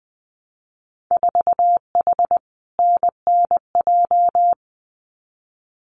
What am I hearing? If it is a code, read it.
Morse code, heard as a keyed tone, about 20 wpm, 700 Hz. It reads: 4H NNJ
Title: Winter Field Day Exchanges Encore 20wpm